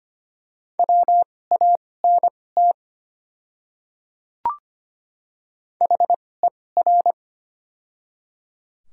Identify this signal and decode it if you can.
Morse code, heard as a keyed tone, about 25 wpm, spaced as for 18 wpm, 700 Hz. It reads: WANT HER
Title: 2x Character Spacing - Top 200 Words - 25wpm